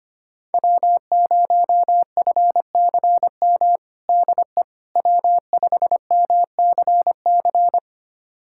Morse code, heard as a keyed tone, about 25 wpm, 700 Hz. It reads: W0FCM DE W5MCC
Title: USA Callsign DE Callsign - 5 Characters - 25wpm